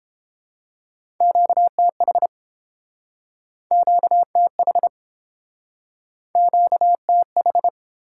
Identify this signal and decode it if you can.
Morse code, heard as a keyed tone, about 32 wpm, 690 Hz. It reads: QTH QTH QTH